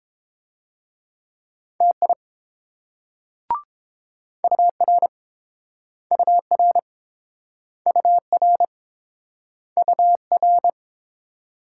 Morse code, heard as a keyed tone, about 30 wpm, 700 Hz. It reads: TI UR UR UR UR